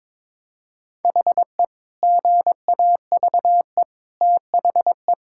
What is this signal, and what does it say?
Morse code, heard as a keyed tone, about 22 wpm, 700 Hz. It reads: HE GAVE THE
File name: practice / 22wpm / Sentences with 4-Word Phrases - 22wpm